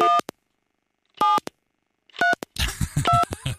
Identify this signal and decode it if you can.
DTMF, heard as telephone touch tones, tones of about 170 ms. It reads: 1 7 32